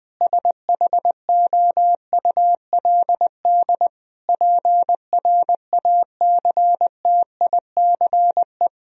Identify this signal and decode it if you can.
Morse code, heard as a keyed tone, about 19 wpm, 700 Hz. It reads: SHOULD PRACTICE
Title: Sentences from Top 1000 Words - Review Part 4 - 20wpm